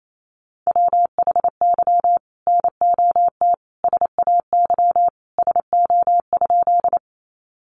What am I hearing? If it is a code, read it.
Morse code, heard as a keyed tone, about 28 wpm, 700 Hz. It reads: WHY NOT SAY SO?
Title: Sentences from Top 600 Words 28wpm